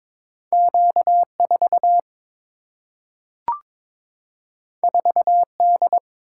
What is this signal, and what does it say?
Morse code, heard as a keyed tone, about 22 wpm, 700 Hz. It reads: Q4 4D
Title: Single-Letter Single-Number 22wpm